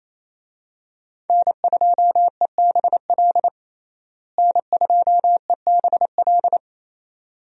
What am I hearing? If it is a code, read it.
Morse code, heard as a keyed tone, about 27 wpm, 700 Hz. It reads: N2EBL N2EBL
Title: USA Call Signs - 5 Characters - Repeated 2x - 28wpm